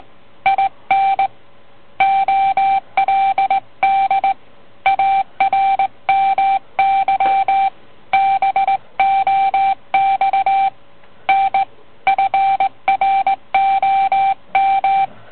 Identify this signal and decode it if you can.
Morse code, heard as a keyed tone, about 18 wpm, 780 Hz. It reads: IN OLD ARMY BOX N FROM